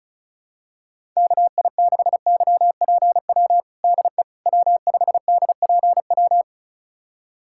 Morse code, heard as a keyed tone, about 36 wpm, 700 Hz. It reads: KI6YPW DE W5DPW
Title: USA Callsign DE Callsign 35wpm